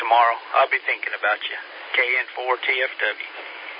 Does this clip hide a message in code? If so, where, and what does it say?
Yes: Emergency Alert System (SAME) tones from 0.7 to 3.3 s, heard as harsh buzzing bursts, with speech.